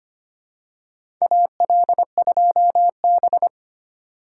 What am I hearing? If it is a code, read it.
Morse code, heard as a keyed tone, about 25 wpm, 700 Hz. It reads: AL2B
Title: USA Call Signs - 4 Characters - 4x Character Spacing - 25wpm